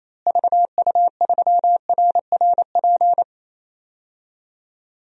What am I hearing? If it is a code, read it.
Morse code, heard as a keyed tone, about 28 wpm, 700 Hz. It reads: VU3RRP
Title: DXCC-100 Call Signs - 6 Characters - Repeated 2x - 28wpm